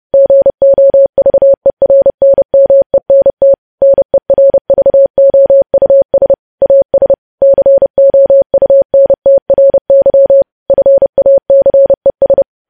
Morse code, heard as a keyed tone, about 30 wpm, 550 Hz. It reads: GOVERNMENT NERVOUS AS COUNTRY FACES
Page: news headlines in morse code at 30 wpm